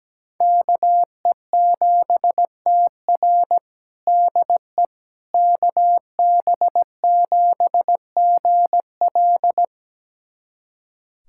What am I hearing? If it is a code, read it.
Morse code, heard as a keyed tone, about 17 wpm, 700 Hz. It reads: KE7TR DE KB7GL
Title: USA Callsign DE Callsign - 5 Characters - 17wpm